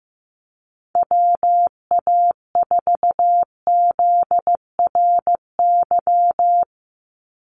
Morse code, heard as a keyed tone, about 15 wpm, 700 Hz. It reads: WA4ZRY
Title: USA Call Signs 15wpm